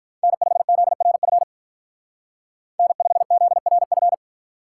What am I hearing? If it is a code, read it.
Morse code, heard as a keyed tone, about 46 wpm, 700 Hz. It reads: N5ZRF N5ZRF